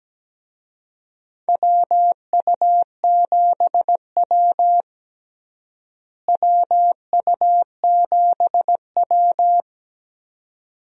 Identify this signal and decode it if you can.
Morse code, heard as a keyed tone, about 17 wpm, 700 Hz. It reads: WU7W WU7W
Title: USA Call Signs - 4 Characters - Repeated 2x - 17wpm